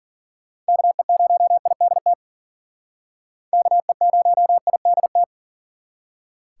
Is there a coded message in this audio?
Yes: Morse code from 0.7 to 5.2 s, heard as a keyed tone, about 51 wpm, 700 Hz. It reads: KE0IDT KE0IDT